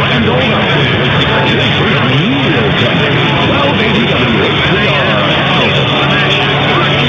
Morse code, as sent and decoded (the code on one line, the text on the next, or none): none